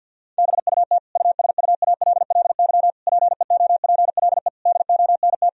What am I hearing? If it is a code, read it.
Morse code, heard as a keyed tone, about 50 wpm, 700 Hz. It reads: BUT USUALLY PEOPLE DONT
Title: Sentences from Top 1000 Words - Review Part 3 - 50wpm